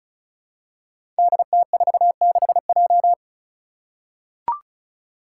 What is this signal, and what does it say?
Morse code, heard as a keyed tone, about 36 wpm, 700 Hz. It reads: DT46J
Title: Five Challenging Letter-Number Combinations 35wpm